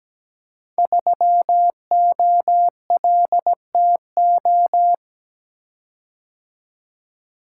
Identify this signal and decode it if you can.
Morse code, heard as a keyed tone, about 17 wpm, 700 Hz. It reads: 3OLTO
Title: Code Groups - 5 Letter:Numbers - 17wpm